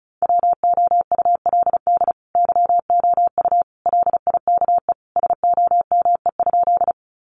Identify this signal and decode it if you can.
Morse code, heard as a keyed tone, about 34 wpm, 700 Hz. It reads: WOULD YOU LIKE SOME?
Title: Sentences from Top 100 Words 35wpm